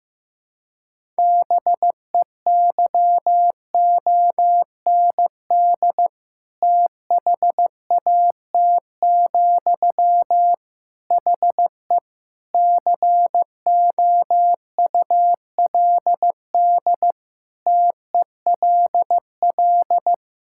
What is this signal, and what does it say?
Morse code, heard as a keyed tone, about 15 wpm, 700 Hz. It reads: BEYOND THAT, HE COULD TELL
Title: Sentences from Top 1000 Words - Review Part 6 - 15wpm